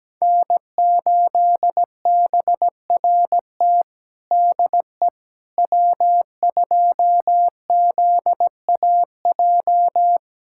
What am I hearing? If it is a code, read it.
Morse code, heard as a keyed tone, about 17 wpm, 700 Hz. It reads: N8BRT DE W2ZAJ
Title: USA Callsign DE Callsign - 5 Characters - 17wpm